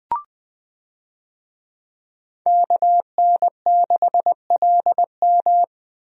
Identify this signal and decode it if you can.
Morse code, heard as a keyed tone, about 20 wpm, 700 Hz. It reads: KN6LM
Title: USA Call Signs - 5 Characters - Repeated 3x - 20wpm